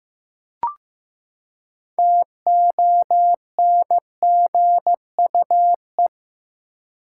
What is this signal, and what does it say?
Morse code, heard as a keyed tone, about 15 wpm, 700 Hz. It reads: TONGUE
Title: Six Letter Words - 15wpm